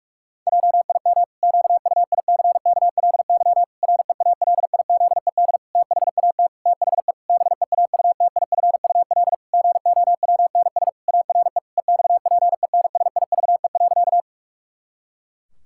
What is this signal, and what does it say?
Morse code, heard as a keyed tone, about 44 wpm, 700 Hz. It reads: JIM QUICKLY REALIZED THAT THE BEAUTIFUL GOWNS ARE EXPENSIVE.
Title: Pangram Challenge v2 - 45wpm